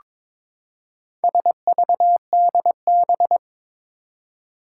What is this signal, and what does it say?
Morse code, heard as a keyed tone, about 22 wpm, 700 Hz. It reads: SVDB
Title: Four Challenging Letter-Number Combinations 22wpm